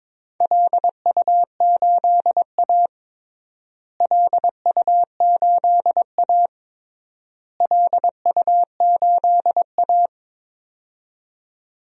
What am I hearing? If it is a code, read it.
Morse code, heard as a keyed tone, about 22 wpm, 700 Hz. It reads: LU8A LU8A LU8A
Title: DXCC-100 Call Signs - 4 Characters - Repeated 3x - 22wpm